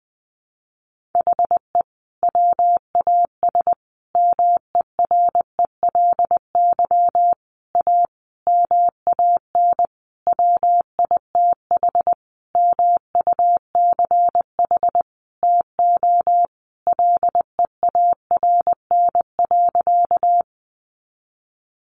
Morse code, heard as a keyed tone, about 20 wpm, 700 Hz. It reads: HE WAS MERELY A MAN WITH MUCH TO LEARN.
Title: Sentences from Top 1400 Words 20wpm